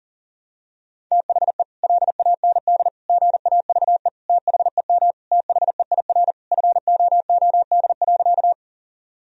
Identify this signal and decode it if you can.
Morse code, heard as a keyed tone, about 40 wpm, 700 Hz. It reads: THE LAND GAVE THEM THEIR FOOD.
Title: Sentences from Top 1000 Words - Review Part 1 - 40wpm